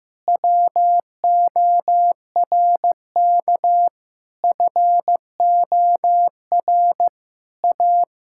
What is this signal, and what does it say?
Morse code, heard as a keyed tone, about 15 wpm, 700 Hz. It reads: WORK FOR A
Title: Sentences from Top 1000 Words - Review Part 6 - 15wpm